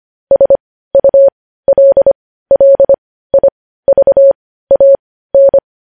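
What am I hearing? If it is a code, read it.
Morse code, heard as a keyed tone, about 25 wpm, spaced as for 15 wpm, 550 Hz. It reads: SULLIVAN
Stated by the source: News headlines sent in Morse code at 15 wpm.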